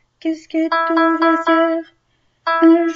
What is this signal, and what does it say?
Speech. There is no code.